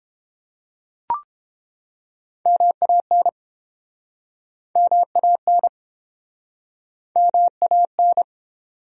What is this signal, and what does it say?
Morse code, heard as a keyed tone, about 31 wpm, 700 Hz. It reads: MAN MAN MAN